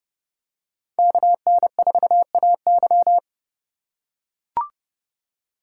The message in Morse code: KN4AY

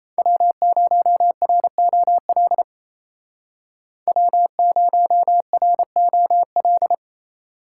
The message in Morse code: W0ROL W0ROL

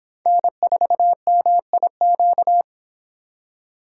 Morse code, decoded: N4MIQ

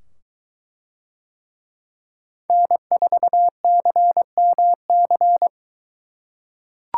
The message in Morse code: N4CMC

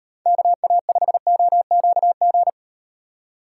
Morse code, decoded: KA5OQG

